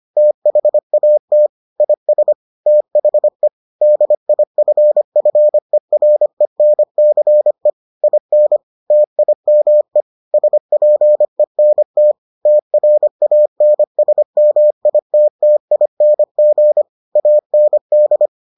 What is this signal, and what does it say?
Morse code, heard as a keyed tone, about 25 wpm, 600 Hz. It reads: THAT IS THE DIFFERENCE IN TIME SPENT TRANSMITTING AND